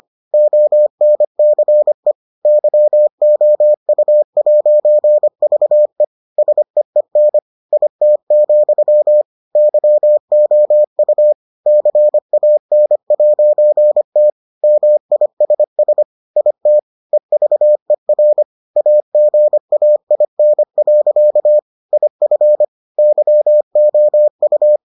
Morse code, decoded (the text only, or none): ONCE YOU'VE SEEN IT, YOU CAN'T MISS IT EVER AGAIN. IF YOU